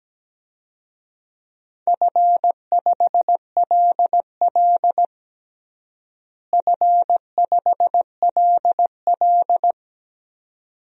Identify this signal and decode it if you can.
Morse code, heard as a keyed tone, about 17 wpm, 700 Hz. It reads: F5LL F5LL
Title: DXCC-100 Call Signs - 4 Characters - Repeated 2x - 17wpm